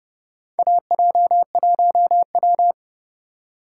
Morse code, heard as a keyed tone, about 30 wpm, 700 Hz. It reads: AJ1W
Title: USA Call Signs - 4 Characters - 2x Character Spacing - 30wpm